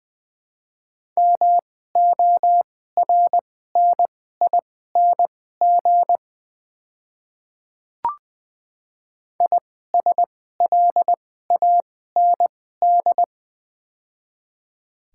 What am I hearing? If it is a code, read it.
Morse code, heard as a keyed tone, about 20 wpm, spaced as for 14 wpm, 700 Hz. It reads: MORNING ISLAND
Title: 2x Character Spacing - Top 500 Words - 20wpm